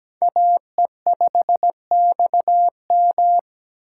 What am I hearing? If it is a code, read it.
Morse code, heard as a keyed tone, about 17 wpm, 700 Hz. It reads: AE5XM
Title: USA Call Signs - 2x Character Spacing - 17wpm